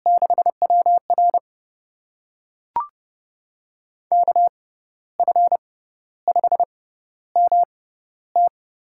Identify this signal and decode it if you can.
Morse code, heard as a keyed tone, about 30 wpm, spaced as for 13 wpm, 700 Hz. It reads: 6WR KF5MT
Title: USA Call Signs - 5 Characters - 6x Character Spacing - 30wpm